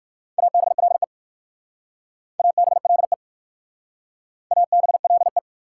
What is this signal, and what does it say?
Morse code, heard as a keyed tone, about 57 wpm, 690 Hz. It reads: ABLE ABLE ABLE